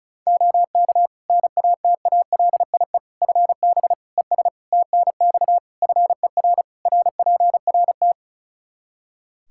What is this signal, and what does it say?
Morse code, heard as a keyed tone, about 34 wpm, 700 Hz. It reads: OK NATALIE FB ES TNX FER RPRT
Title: Practice QSO - Exchange 2 - Info - 35wpm